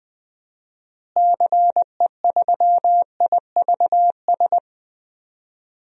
Morse code, heard as a keyed tone, about 21 wpm, 700 Hz. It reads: CE3IVS
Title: DXCC-100 Call Signs - 6 Characters - Repeated 2x - 20wpm